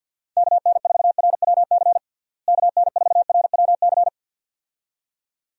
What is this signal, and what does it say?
Morse code, heard as a keyed tone, about 52 wpm, 700 Hz. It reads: KN4RWC KN4RWC
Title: USA Call Signs - 6 Characters - Repeated 2x - 50wpm